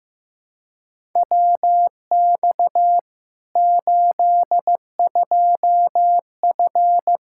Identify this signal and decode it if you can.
Morse code, heard as a keyed tone, about 15 wpm, 700 Hz. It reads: WX 82F
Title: Practice QSO - Exchange 2 - Info - 15wpm